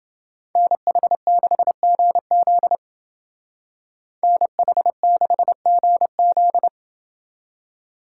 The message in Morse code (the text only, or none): NH6GZ NH6GZ